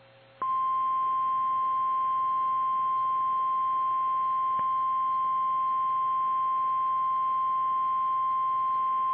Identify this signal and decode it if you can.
Emergency Alert System (SAME) tones (the 1050 Hz attention tone). It is heard from 1.1 to 8.3 s.
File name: eas / KKPRKXPN / KKPRKXPN